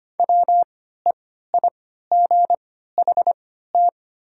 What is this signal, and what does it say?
Morse code, heard as a keyed tone, about 25 wpm, spaced as for 14 wpm, 700 Hz. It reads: WEIGHT
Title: 3x Character Spacing - Top 500 Words - 25wpm